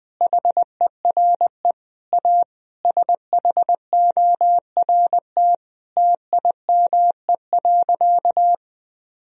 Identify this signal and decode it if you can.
Morse code, heard as a keyed tone, about 20 wpm, 700 Hz. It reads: HERE A SHORT TIME.
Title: Sentences from Top 1000 Words - Review Part 2 - 20wpm